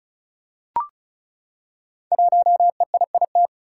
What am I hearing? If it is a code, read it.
Morse code, heard as a keyed tone, about 36 wpm, 700 Hz. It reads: 1EIIT
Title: Five Challenging Letter-Number Combinations 35wpm